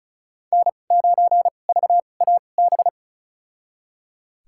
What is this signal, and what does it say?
Morse code, heard as a keyed tone, about 36 wpm, spaced as for 25 wpm, 700 Hz. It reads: N9VAB
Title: USA Call Signs - 5 Characters - 2x Character Spacing - 35wpm